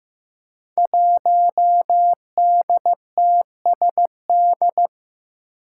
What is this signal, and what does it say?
Morse code, heard as a keyed tone, about 15 wpm, 700 Hz. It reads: 1DTSD
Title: Five Challenging Letter-Number Combinations 15wpm